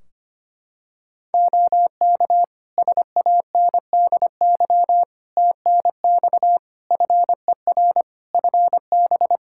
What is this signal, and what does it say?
Morse code, heard as a keyed tone, about 25 wpm, 700 Hz. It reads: OK SANDY TNX FER FB
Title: K1USN SST QSO Practice 25wpm